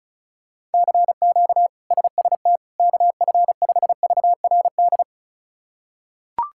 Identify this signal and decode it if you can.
Morse code, heard as a keyed tone, about 36 wpm, 700 Hz. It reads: CQ SST KF5VRD
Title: K1USN SST QSO Practice - 35wpm